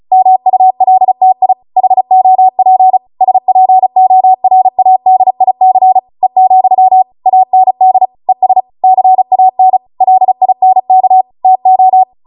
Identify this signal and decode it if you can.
Morse code, heard as a keyed tone, about 36 wpm, 750 Hz. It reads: MULTI HOP SPORADIC E, AND ES CAN LINK TO